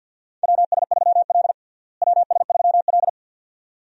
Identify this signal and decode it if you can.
Morse code, heard as a keyed tone, about 52 wpm, 700 Hz. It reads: WS3L WS3L
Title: USA Call Signs - 4 Characters - Repeated 2x - 50wpm